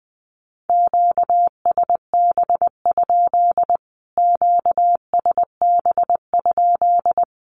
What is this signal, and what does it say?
Morse code, heard as a keyed tone, about 20 wpm, 700 Hz. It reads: QSB? QSB?
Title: Q-Codes Repeated 3x 20wpm